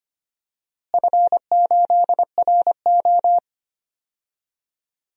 Morse code, heard as a keyed tone, about 25 wpm, 700 Hz. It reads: F8RO
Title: DXCC-100 Call Signs - 4 Characters - Repeated 3x - 25wpm